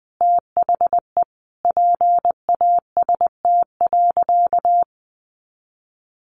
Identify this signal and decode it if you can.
Morse code, heard as a keyed tone, about 20 wpm, 700 Hz. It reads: THE PAST.
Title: Sentences from Top 1100 Words 20wpm